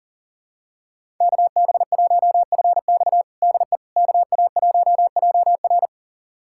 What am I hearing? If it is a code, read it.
Morse code, heard as a keyed tone, about 40 wpm, 700 Hz. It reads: KB1FX DE KA1JR
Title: USA Callsign DE Callsign - 5 Characters - 40wpm